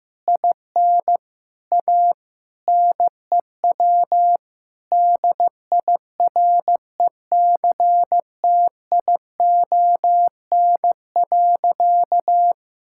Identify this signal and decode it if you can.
Morse code, heard as a keyed tone, about 15 wpm, 700 Hz. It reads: IN A NEW DIRECTION.
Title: Sentences from Top 1000 Words - Review Part 5 - 15wpm